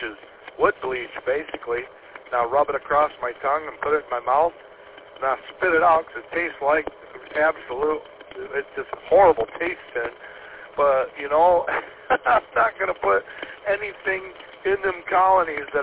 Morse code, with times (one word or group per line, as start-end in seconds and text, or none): none